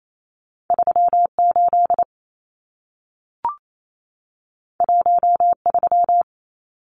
Morse code, heard as a keyed tone, about 28 wpm, 700 Hz. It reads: 38 13
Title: Numbers - 2-Digit 28wpm